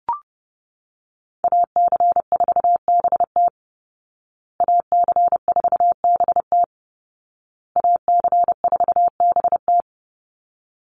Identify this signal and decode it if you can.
Morse code, heard as a keyed tone, about 30 wpm, 700 Hz. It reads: AC4BT AC4BT AC4BT
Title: Call Signs Repeated 2x - 30wpm